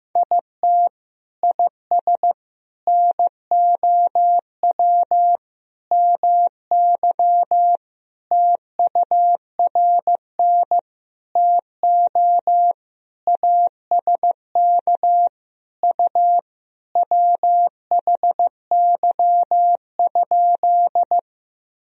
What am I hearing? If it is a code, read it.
Morse code, heard as a keyed tone, about 15 wpm, 700 Hz. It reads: IT IS NOW MY TURN TO ASK U WHY?
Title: Sentences from Top 200 Words and Abbreviations 15wpm